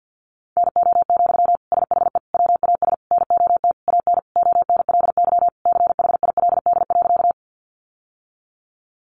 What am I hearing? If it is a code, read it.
Morse code, heard as a keyed tone, about 50 wpm, 700 Hz. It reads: NO, SHE WAS NOT AN ONLY CHILD.
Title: Sentences from Top 800 Words 50wpm